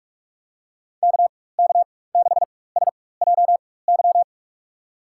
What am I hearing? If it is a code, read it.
Morse code, heard as a keyed tone, about 43 wpm, spaced as for 21 wpm, 700 Hz. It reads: KK6SJY
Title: USA Call Signs - 6 Characters - 4x Character Spacing - 45wpm